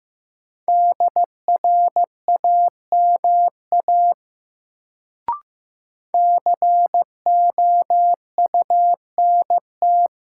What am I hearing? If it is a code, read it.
Morse code, heard as a keyed tone, about 15 wpm, 700 Hz. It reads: DRAMA COUNT
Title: Sentences from Top 1500 Words - New Word Drill - 15wpm